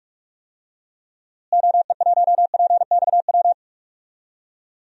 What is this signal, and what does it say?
Morse code, heard as a keyed tone, about 42 wpm, 700 Hz. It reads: OE1PXW
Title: DXCC-100 Call Signs - 6 Characters - Repeated 2x - 45wpm